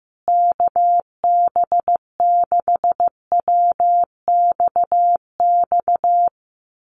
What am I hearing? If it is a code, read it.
Morse code, heard as a keyed tone, about 15 wpm, 700 Hz. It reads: KB6WXX
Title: USA Call Signs Repeated 2x 15wpm